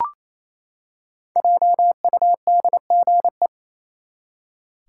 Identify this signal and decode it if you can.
Morse code, heard as a keyed tone, about 28 wpm, 700 Hz. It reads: JUDGE
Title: Five Letter Words - 28wpm